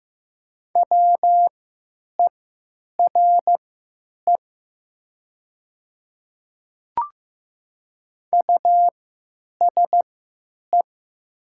Morse code, heard as a keyed tone, about 15 wpm, spaced as for 9 wpm, 700 Hz. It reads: WERE USE
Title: 3x Character Spacing - Top 100 Words - 15wpm